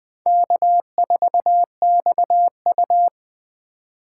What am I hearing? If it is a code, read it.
Morse code, heard as a keyed tone, about 24 wpm, 700 Hz. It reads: K4XU
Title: Speed Racing - USA Call Signs 20wpm